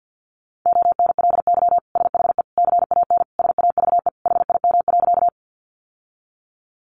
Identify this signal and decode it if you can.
Morse code, heard as a keyed tone, about 50 wpm, 700 Hz. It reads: ONLY SHE CAN SAVE HIM.
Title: Sentences from Top 1100 Words 50wpm